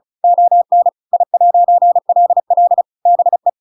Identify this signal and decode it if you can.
Morse code, heard as a keyed tone, about 34 wpm, 700 Hz. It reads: ON I'LL BE
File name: Pinocchio / Pinocchio ch8 35 35 ik1hge